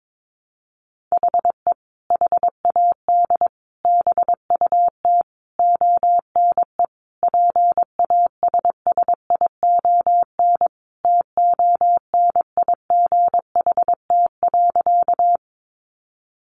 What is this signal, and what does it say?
Morse code, heard as a keyed tone, about 22 wpm, 700 Hz. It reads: HE HAD BUT ONE PASSION TONIGHT.
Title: Sentences from Top 1300 Words 22wpm